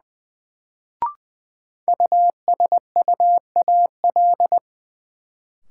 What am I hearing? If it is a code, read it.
Morse code, heard as a keyed tone, about 20 wpm, 700 Hz. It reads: USUAL
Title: Sentences from Top 400 Words - New Word Drill - 20wpm